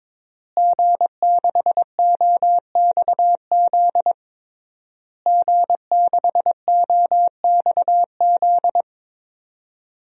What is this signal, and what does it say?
Morse code, heard as a keyed tone, about 22 wpm, 700 Hz. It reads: G6OXZ G6OXZ
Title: DXCC-100 Call Signs - 5 Characters - Repeated 3x - 22wpm